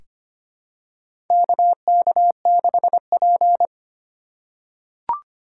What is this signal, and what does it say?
Morse code, heard as a keyed tone, about 25 wpm, 700 Hz. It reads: KK6P
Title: USA Call Signs - 4 Characters - 2x Character Spacing - 25wpm